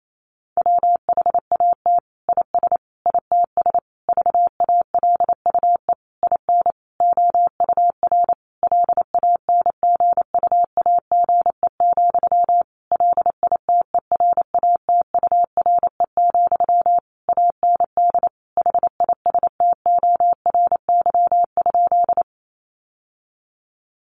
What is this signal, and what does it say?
Morse code, heard as a keyed tone, about 28 wpm, 700 Hz. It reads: WHAT IS ITS VALUE IN OUR LANGUAGE, LITERATURE, AND HISTORY?
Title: Sentences from Top 600 Words 28wpm